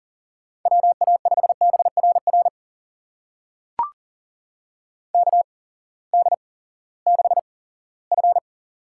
Morse code, heard as a keyed tone, about 43 wpm, spaced as for 20 wpm, 690 Hz. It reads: WA5BRR KD6F
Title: USA Call Signs - 8x Character Spacing - 40wpm